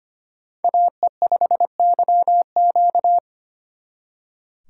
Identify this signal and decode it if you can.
Morse code, heard as a keyed tone, about 32 wpm, 700 Hz. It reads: AE5YQ